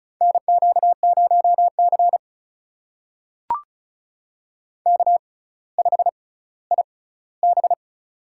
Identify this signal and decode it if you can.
Morse code, heard as a keyed tone, about 36 wpm, spaced as for 18 wpm, 700 Hz. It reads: NQ0C K5IB
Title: USA Call Signs - 4 Characters - 6x Character Spacing - 35wpm